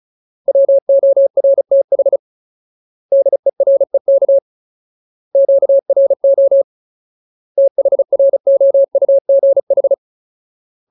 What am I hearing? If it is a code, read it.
Morse code, heard as a keyed tone, about 34 wpm, 550 Hz. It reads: WORTH DEREK QRO THROUGH
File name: Ultimate Word List / 35WPM Ultimate